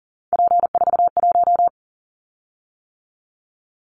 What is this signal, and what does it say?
Morse code, heard as a keyed tone, about 40 wpm, 700 Hz. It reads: P41
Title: Callsign Prefixes for Top 100 DXCC - 40wpm